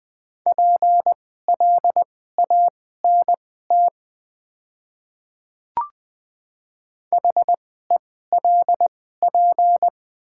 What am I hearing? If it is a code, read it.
Morse code, heard as a keyed tone, about 20 wpm, spaced as for 14 wpm, 700 Hz. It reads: PLANT HELP
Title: 2x Character Spacing - Top 500 Words - 20wpm